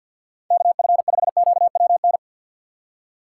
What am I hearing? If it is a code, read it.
Morse code, heard as a keyed tone, about 52 wpm, 700 Hz. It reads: KF5QWN
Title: USA Call Signs - 6 Characters - 4x Character Spacing - 50wpm